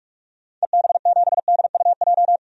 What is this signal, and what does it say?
Morse code, heard as a keyed tone, about 44 wpm, 700 Hz. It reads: EB7DUJ
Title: DXCC-100 Call Signs - 6 Characters - 45wpm